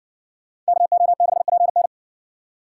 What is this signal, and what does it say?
Morse code, heard as a keyed tone, about 54 wpm, 700 Hz. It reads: KO6PN